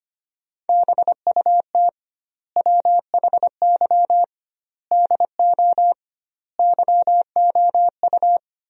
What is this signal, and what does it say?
Morse code, heard as a keyed tone, about 25 wpm, 700 Hz. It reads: BUT WHY DO YOU
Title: 2x Word-Spacing - Sentences from Top 200 Words - 25wpm